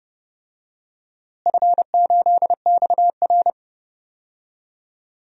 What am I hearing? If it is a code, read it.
Morse code, heard as a keyed tone, about 30 wpm, 700 Hz. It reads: F8XR
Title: DXCC-100 Call Signs - 4 Characters - Repeated 2x - 30wpm